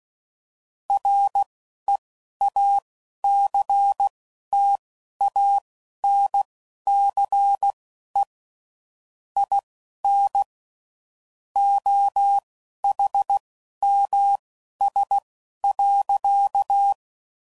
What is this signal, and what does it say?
Morse code, heard as a keyed tone, about 16 wpm, spaced as for 11 wpm, 800 Hz. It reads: REACTANCE IN OHMS.
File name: CW Code Groups @10WPM / Technical Subjects / XL